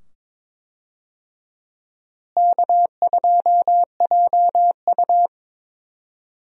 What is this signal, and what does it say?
Morse code, heard as a keyed tone, about 22 wpm, 700 Hz. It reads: K2JU
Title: USA Call Signs - 4 Characters - Repeated 2x - 22wpm